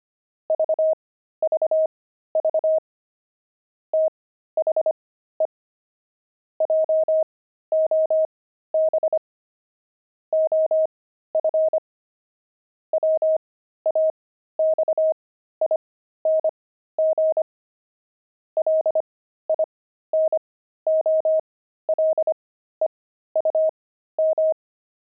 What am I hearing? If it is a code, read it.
Morse code, heard as a keyed tone, about 25 wpm, spaced as for 13 wpm, 630 Hz. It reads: VVV THE JOB OF WAXING LINOLEUM
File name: pangram-fs-13-cs-25-linoleum